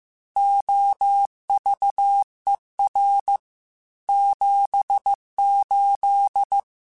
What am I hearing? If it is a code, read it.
Morse code, heard as a keyed tone, about 15 wpm, 800 Hz. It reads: OVER 78
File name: QSOs / QSOs-mp3s / WA7QOX 2@13